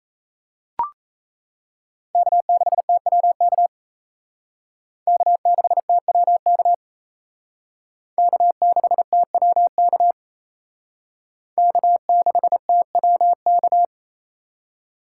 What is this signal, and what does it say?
Morse code, heard as a keyed tone, about 35 wpm, 700 Hz. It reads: K6TWK K6TWK K6TWK K6TWK